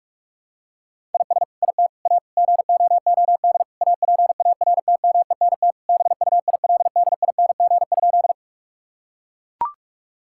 Morse code, heard as a keyed tone, about 41 wpm, 700 Hz. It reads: IS IT A GOOD APARTMENT BUILDING?
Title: Sentences from Top 1000 Words - Review Part 6 - 45wpm